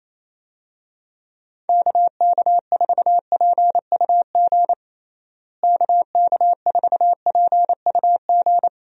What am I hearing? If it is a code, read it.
Morse code, heard as a keyed tone, about 28 wpm, 700 Hz. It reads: KK4PUG KK4PUG
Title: USA Call Signs - 6 Characters - Repeated 2x - 28wpm